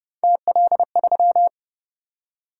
Morse code, heard as a keyed tone, about 29 wpm, 700 Hz. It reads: TL3